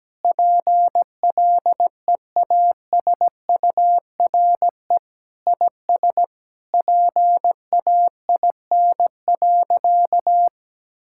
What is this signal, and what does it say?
Morse code, heard as a keyed tone, about 17 wpm, 700 Hz. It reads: PLEASURE IS PAIN.
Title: Sentences from Top 1600 Words 17wpm